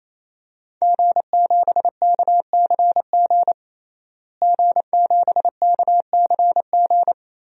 Morse code, heard as a keyed tone, about 28 wpm, 700 Hz. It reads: G7KCG G7KCG
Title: DXCC-100 Call Signs - 5 Characters - Repeated 3x - 28wpm